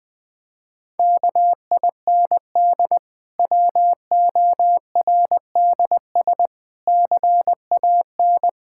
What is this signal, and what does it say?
Morse code, heard as a keyed tone, about 20 wpm, 700 Hz. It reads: KIND WORDS CAN
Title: Difficult Sentences with Hints - 20wpm